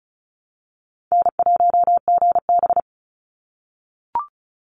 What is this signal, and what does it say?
Morse code, heard as a keyed tone, about 36 wpm, 700 Hz. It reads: N1GB
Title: NA SOTA Chaser Callsigns 35wpm